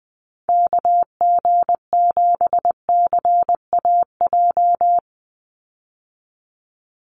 Morse code, heard as a keyed tone, about 20 wpm, 700 Hz. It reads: KG7CAJ
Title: USA Call Signs 20wpm